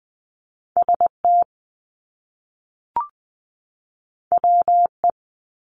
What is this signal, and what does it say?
Morse code, heard as a keyed tone, about 20 wpm, 700 Hz. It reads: ST WE
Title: Top Two Letter Combinations in Words 20wpm